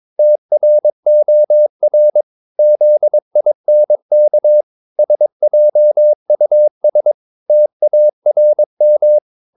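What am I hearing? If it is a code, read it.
Morse code, heard as a keyed tone, about 22 wpm, 600 Hz. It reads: TROR ZINK SJUS TARM
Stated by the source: Morse practice of short Danish words at 22 wpm, from CW med Gnister.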